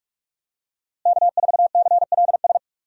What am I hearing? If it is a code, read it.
Morse code, heard as a keyed tone, about 43 wpm, 700 Hz. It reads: K4CLS
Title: USA Call Signs - 5 Characters - Repeated 2x - 45wpm